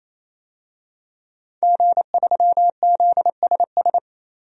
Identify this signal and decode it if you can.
Morse code, heard as a keyed tone, about 28 wpm, 700 Hz. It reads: G3ZSS
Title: DXCC-100 Call Signs - 5 Characters - 28wpm